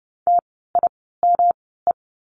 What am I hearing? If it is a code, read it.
Morse code, heard as a keyed tone, about 30 wpm, spaced as for 17 wpm, 700 Hz. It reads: TIME
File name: course / 30:10wpm / 115 Lesson 19f - Letter w - Words 30wpm